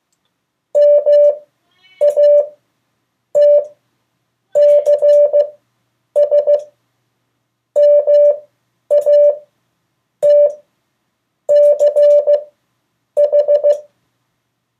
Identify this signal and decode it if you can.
Morse code, heard as a keyed tone, about 15 wpm, spaced as for 7 wpm, 580 Hz. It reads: MATCSMATCH
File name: 2of4-words-1